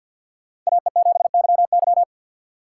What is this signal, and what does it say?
Morse code, heard as a keyed tone, about 52 wpm, 700 Hz. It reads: AE7YY
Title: USA Call Signs - 5 Characters - 4x Character Spacing - 50wpm